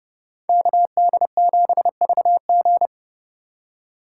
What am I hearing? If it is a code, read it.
Morse code, heard as a keyed tone, about 39 wpm, 700 Hz. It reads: KD7VG